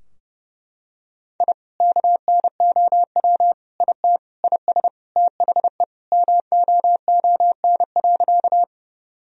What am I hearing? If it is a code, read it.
Morse code, heard as a keyed tone, about 32 wpm, 700 Hz. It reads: I KNOW IT IS THE MOON.
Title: Sentences from Top 1000 Words - Review Part 3 - 30wpm